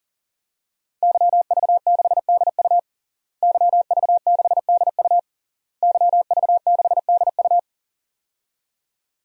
Morse code, heard as a keyed tone, about 40 wpm, 700 Hz. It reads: YV6DU YV6DU YV6DU
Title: DXCC-100 Call Signs - 5 Characters - Repeated 3x - 40wpm